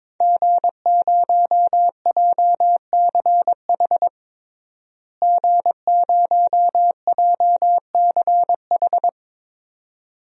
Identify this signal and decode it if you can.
Morse code, heard as a keyed tone, about 22 wpm, 700 Hz. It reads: G0JCH G0JCH
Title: DXCC-100 Call Signs - 5 Characters - Repeated 2x - 22wpm